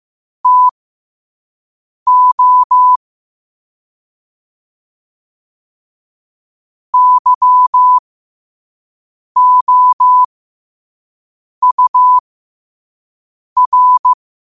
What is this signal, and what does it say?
Morse code, heard as a keyed tone, about 15 wpm, spaced as for 5 wpm, 1000 Hz. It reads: TO YOUR
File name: qotdfiles / 5wpm